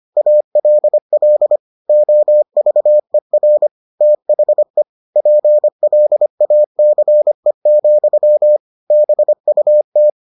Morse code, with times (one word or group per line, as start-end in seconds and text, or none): 0.2-1.6 ALL
1.9-3.7 OVER
4.0-4.8 THE
5.2-8.6 PLACE,
8.9-10.1 BUT